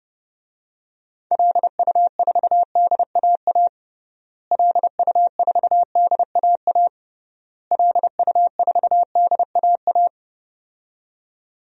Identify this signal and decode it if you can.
Morse code, heard as a keyed tone, about 32 wpm, 700 Hz. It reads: LU4DAA LU4DAA LU4DAA
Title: DXCC-100 Call Signs - 6 Characters - Repeated 3x - 30wpm